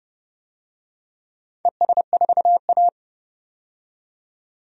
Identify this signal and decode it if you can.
Morse code, heard as a keyed tone, about 30 wpm, 700 Hz. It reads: ES4A
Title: DXCC-100 Call Signs - 4 Characters - 30wpm